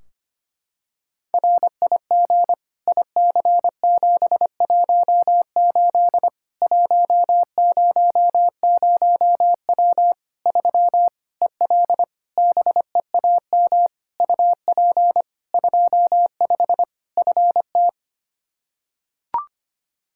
Morse code, heard as a keyed tone, about 25 wpm, 700 Hz. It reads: RIG IC718 100W 3 EL BEAM UP 25 FT
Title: Practice QSO - Exchange 2 - Info - 25wpm